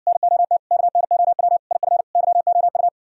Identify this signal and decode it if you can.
Morse code, heard as a keyed tone, about 66 wpm, 700 Hz. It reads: NOT KNOW IF YOU